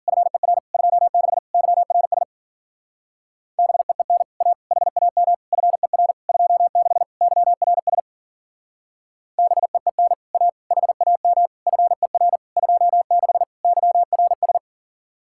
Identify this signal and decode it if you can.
Morse code, heard as a keyed tone, about 49 wpm, 700 Hz. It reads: FER 26 YRS BEEN A HAM FER 26 YRS BEEN A HAM FER 26 YRS
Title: Speed-Racing Common QSO Phrases - 40wpm